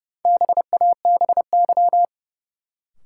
Morse code, heard as a keyed tone, about 31 wpm, 700 Hz. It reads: BABY
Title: Sentences from Top 1300 Words - New Word Drill Alt - 30wpm